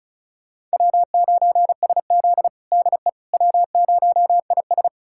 Morse code, heard as a keyed tone, about 36 wpm, 700 Hz. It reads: W9SZ DE W0IS
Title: USA Callsign DE Callsign - 4 Characters - 35wpm